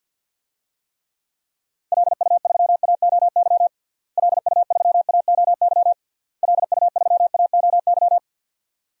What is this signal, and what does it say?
Morse code, heard as a keyed tone, about 48 wpm, 700 Hz. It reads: LU3AOY LU3AOY LU3AOY
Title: DXCC-100 Call Signs - 6 Characters - Repeated 3x - 50wpm